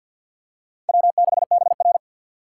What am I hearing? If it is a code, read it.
Morse code, heard as a keyed tone, about 45 wpm, 690 Hz. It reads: W6BR
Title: USA Call Signs - 4 Characters - 6x Character Spacing - 50wpm